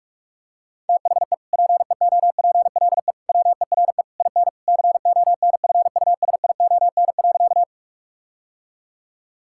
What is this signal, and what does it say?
Morse code, heard as a keyed tone, about 41 wpm, 700 Hz. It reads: THE PEOPLE WERE IN CONFUSION.
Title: Sentences from Top 1600 Words 45wpm